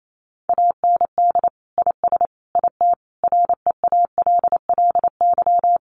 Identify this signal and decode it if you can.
Morse code, heard as a keyed tone, about 28 wpm, 700 Hz. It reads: AND IS IT REALLY
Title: Sentences from Top 600 Words 28wpm